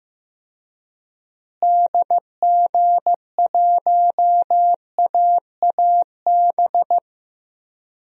Morse code, heard as a keyed tone, about 15 wpm, 700 Hz. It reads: DG1AAB
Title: DXCC-100 Call Signs - 6 Characters - Repeated 2x - 15wpm